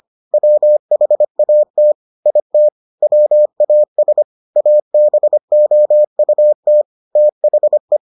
Morse code, heard as a keyed tone, about 25 wpm, 600 Hz. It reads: WHAT IT WAS ABOUT THE